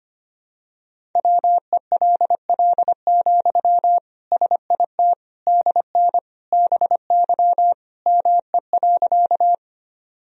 Morse code, heard as a keyed tone, about 25 wpm, 700 Hz. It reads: WELL, SIT DN BY ME.
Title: Sentences from Top 500 Words and Abbreviations 25wpm